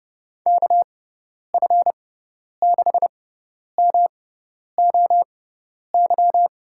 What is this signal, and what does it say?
Morse code, heard as a keyed tone, about 30 wpm, spaced as for 10 wpm, 700 Hz. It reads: KF6MOY
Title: USA Call Signs - 6 Characters - 6x Character Spacing - 30wpm